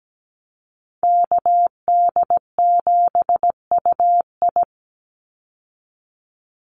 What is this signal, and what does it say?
Morse code, heard as a keyed tone, about 17 wpm, 700 Hz. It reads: KD7UI
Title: Call Signs Repeated 2x - 17wpm